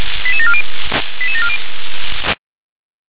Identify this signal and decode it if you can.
Five-tone selective call (ZVEI), heard as quick runs of five tones, tones of about 70 ms. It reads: 88740 88740
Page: SDR/ZVEI